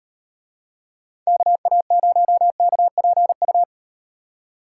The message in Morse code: KA0KPU

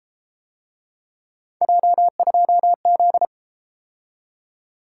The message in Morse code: J2Z

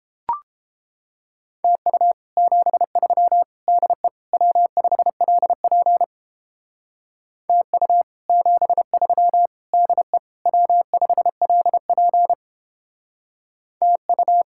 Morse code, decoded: TU 73 DE W5LP TU 73 DE W5LP TU

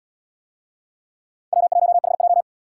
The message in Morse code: W1HJ

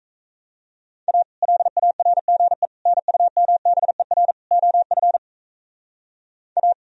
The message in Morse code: A LARGE NUMBER OF A

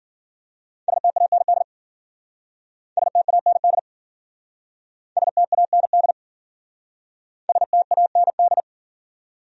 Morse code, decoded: STAND STAND STAND STAND

